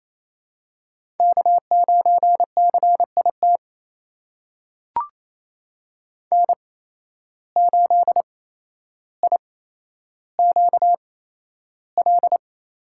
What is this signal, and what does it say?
Morse code, heard as a keyed tone, about 28 wpm, spaced as for 12 wpm, 700 Hz. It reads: K9CIT N8IQL
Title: USA Call Signs - 5 Characters - 8x Character Spacing - 28wpm